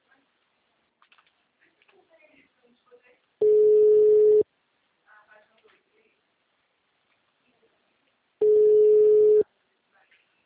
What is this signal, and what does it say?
Ringback tone (Europe, 425 Hz), with speech.
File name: out- -2007- - - .66838